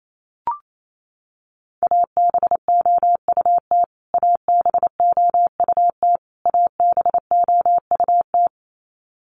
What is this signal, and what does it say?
Morse code, heard as a keyed tone, about 28 wpm, 700 Hz. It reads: ABOUT ABOUT ABOUT
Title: Top 200 Words Repeated 28wpm